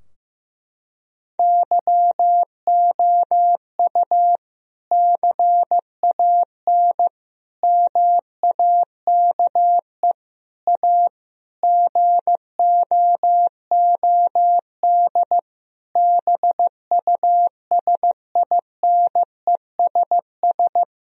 Morse code, heard as a keyed tone, about 15 wpm, 700 Hz. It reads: YOU CAN MAKE A GOOD BUSINESS
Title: Sentences from Top 1000 Words - Review Part 4 - 15wpm